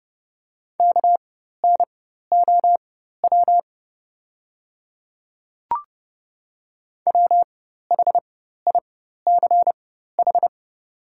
Morse code, heard as a keyed tone, about 30 wpm, spaced as for 14 wpm, 700 Hz. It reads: KNOW WHICH
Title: 4x Character Spacing - Top 200 Words - 30wpm